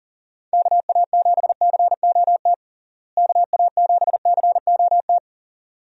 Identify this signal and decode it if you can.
Morse code, heard as a keyed tone, about 40 wpm, 700 Hz. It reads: KA7COT KA7COT
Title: USA Call Signs - 6 Characters - Repeated 2x - 40wpm